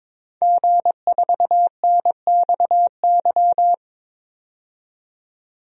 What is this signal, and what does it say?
Morse code, heard as a keyed tone, about 22 wpm, 700 Hz. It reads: G4NXY
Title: DXCC-100 Call Signs - 5 Characters - 22wpm